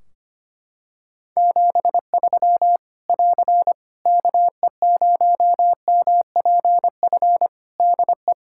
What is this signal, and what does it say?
Morse code, heard as a keyed tone, about 25 wpm, 700 Hz. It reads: 73 <AR> KE0MPF DE
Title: Practice QSO - Exchange 4 - Ending - 25wpm